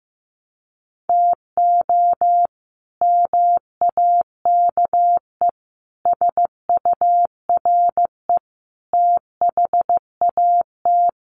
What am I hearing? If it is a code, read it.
Morse code, heard as a keyed tone, about 15 wpm, 700 Hz. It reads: TO MAKE SURE THAT
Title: Sets of 5 Words Encore2 15wpm